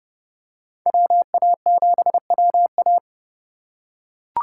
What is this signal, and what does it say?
Morse code, heard as a keyed tone, about 31 wpm, 700 Hz. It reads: WA7WA
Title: USA Call Signs - 5 Characters - 2x Character Spacing - 30wpm